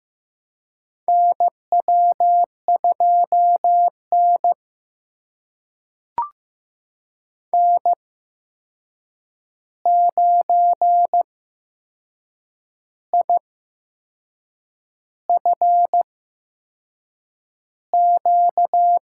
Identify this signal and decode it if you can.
Morse code, heard as a keyed tone, about 15 wpm, spaced as for 6 wpm, 700 Hz. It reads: NW2N N9IFQ
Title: USA Call Signs - 8x Character Spacing - 15wpm